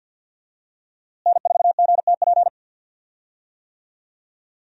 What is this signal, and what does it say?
Morse code, heard as a keyed tone, about 51 wpm, 700 Hz. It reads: N4GTP
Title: USA Call Signs - 5 Characters - Repeated 2x - 50wpm